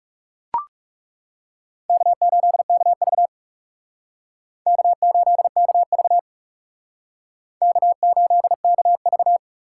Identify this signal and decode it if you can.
Morse code, heard as a keyed tone, about 39 wpm, 700 Hz. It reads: K8KV K8KV K8KV